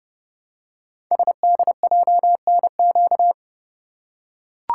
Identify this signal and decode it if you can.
Morse code, heard as a keyed tone, about 30 wpm, 700 Hz. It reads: SDJNQ
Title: Code Groups - 5 Letter:Numbers - 30wpm